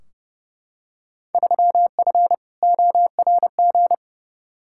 Morse code, heard as a keyed tone, about 30 wpm, 700 Hz. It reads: 3F ORG
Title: Field Day Exchanges - 30wpm